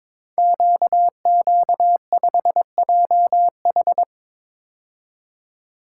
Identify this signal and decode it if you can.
Morse code, heard as a keyed tone, about 22 wpm, 700 Hz. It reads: QQ5JH
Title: Code Groups - 5 Letter:Numbers - 22wpm